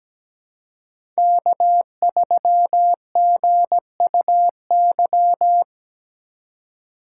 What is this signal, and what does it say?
Morse code, heard as a keyed tone, about 17 wpm, 700 Hz. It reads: K3GUY
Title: USA Call Signs - 2x Character Spacing - 17wpm